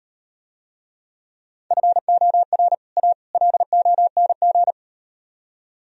Morse code, heard as a keyed tone, about 38 wpm, 700 Hz. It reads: FOR A LONG